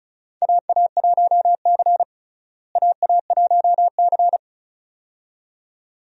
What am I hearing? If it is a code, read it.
Morse code, heard as a keyed tone, about 36 wpm, 700 Hz. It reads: AA1C AA1C
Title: USA Call Signs - 4 Characters - Repeated 2x - 35wpm